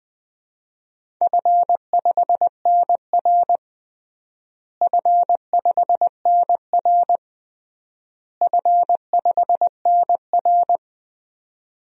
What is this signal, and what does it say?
Morse code, heard as a keyed tone, about 20 wpm, 700 Hz. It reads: F5NR F5NR F5NR
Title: DXCC-100 Call Signs - 4 Characters - Repeated 3x - 20wpm